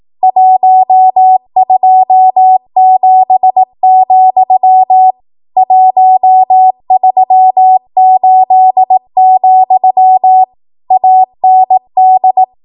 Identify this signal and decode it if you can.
Morse code, heard as a keyed tone, about 18 wpm, 750 Hz. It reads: 127, 138, AND
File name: Archive / 18 WPM / 18WPM